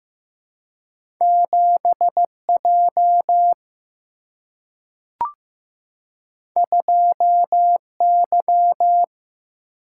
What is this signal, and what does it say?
Morse code, heard as a keyed tone, about 15 wpm, 700 Hz. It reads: 7J 2Y
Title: Single-Letter Single-Number 15wpm